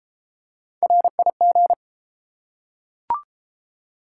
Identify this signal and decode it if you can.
Morse code, heard as a keyed tone, about 30 wpm, 700 Hz. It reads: RIG